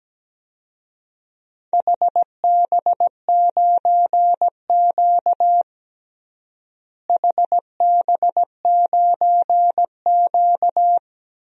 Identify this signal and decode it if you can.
Morse code, heard as a keyed tone, about 17 wpm, 700 Hz. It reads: HB9Q HB9Q
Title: DXCC-100 Call Signs - 4 Characters - Repeated 2x - 17wpm